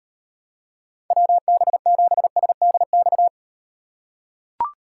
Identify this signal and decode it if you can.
Morse code, heard as a keyed tone, about 38 wpm, 700 Hz. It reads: WB7SDX